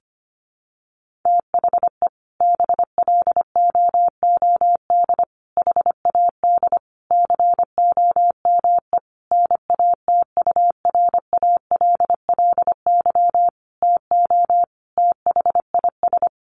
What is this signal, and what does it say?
Morse code, heard as a keyed tone, about 25 wpm, 700 Hz. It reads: THE BLOOD HAD COME NATURALLY TO THIS
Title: Sentences from Top 1300 Words 25wpm